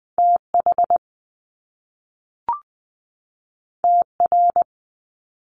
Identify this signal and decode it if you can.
Morse code, heard as a keyed tone, about 20 wpm, 700 Hz. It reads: TH TR
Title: Top Two Letter Combinations in Words 20wpm